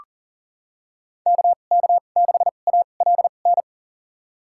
Morse code, heard as a keyed tone, about 40 wpm, spaced as for 29 wpm, 700 Hz. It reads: KK6ALN
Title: USA Call Signs - 6 Characters - 2x Character Spacing - 40wpm